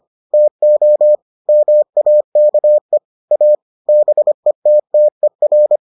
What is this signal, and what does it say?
Morse code, heard as a keyed tone, about 25 wpm, 600 Hz. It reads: TO MAKE A BETTER